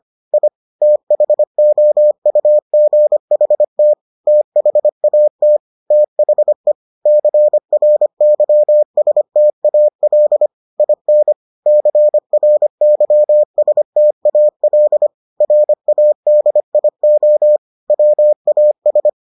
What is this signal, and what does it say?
Morse code, heard as a keyed tone, about 25 wpm, 600 Hz. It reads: I THOUGHT THAT THE CRYSTAL IN CRYSTAL RADIO WAS